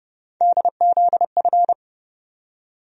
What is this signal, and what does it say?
Morse code, heard as a keyed tone, about 30 wpm, 700 Hz. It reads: DZF
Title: USA Call Sign Postfixes - 3 Letters 30wpm